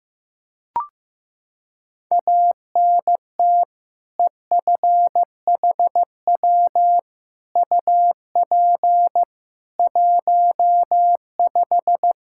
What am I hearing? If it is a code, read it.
Morse code, heard as a keyed tone, about 15 wpm, 700 Hz. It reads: ANT EFHW UP 15
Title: Practice QSO - Exchange 2 - Info - 15wpm